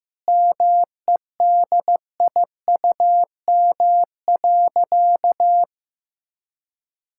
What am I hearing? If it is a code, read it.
Morse code, heard as a keyed tone, about 15 wpm, 700 Hz. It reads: MEDIUM.
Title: Sentences from Top 1000 Words - Review Part 6 - 15wpm